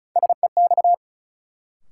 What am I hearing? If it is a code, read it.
Morse code, heard as a keyed tone, about 36 wpm, 700 Hz. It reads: SEX
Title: Sentences from Top 1600 Words - New Word Drill - 35wpm